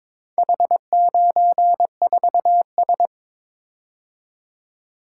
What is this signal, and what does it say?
Morse code, heard as a keyed tone, about 22 wpm, 700 Hz. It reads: H94S
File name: practice / 22wpm / Four Letter-Number Combinations - 22wpm